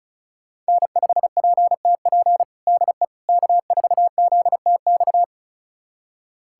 Morse code, heard as a keyed tone, about 36 wpm, 700 Hz. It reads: N5PTP DE K4ZTX
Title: USA Callsign DE Callsign - 5 Characters - 35wpm